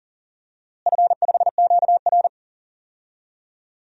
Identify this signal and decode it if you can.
Morse code, heard as a keyed tone, about 38 wpm, 700 Hz. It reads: F5QR
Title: DXCC-100 Call Signs - 4 Characters - Repeated 2x - 40wpm